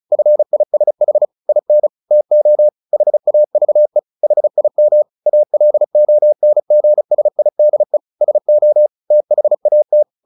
Morse code, heard as a keyed tone, about 35 wpm, 600 Hz. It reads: FISH IN TO HAVE HIM ALONGSIDE SO THAT
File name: the old man and the sea 2 35wpm